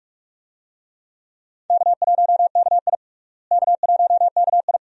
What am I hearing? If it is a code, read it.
Morse code, heard as a keyed tone, about 41 wpm, 700 Hz. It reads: K1KI K1KI
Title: USA Call Signs - 4 Characters - Repeated 2x - 45wpm